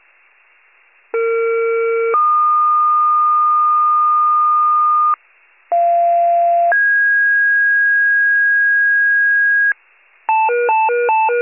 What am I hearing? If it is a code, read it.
Two-tone page, heard as a one-second tone, then a three-second tone, about 460 then 1220 Hz, about 690 then 1650 Hz, with noise. It is heard from 1.1 to 9.7 s.
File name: pager / 21 RESCUE / wcalert